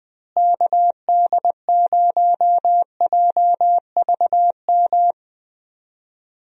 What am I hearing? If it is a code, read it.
Morse code, heard as a keyed tone, about 20 wpm, 700 Hz. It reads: KD0JVM
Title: USA Call Signs - 6 Characters - Repeated 2x - 20wpm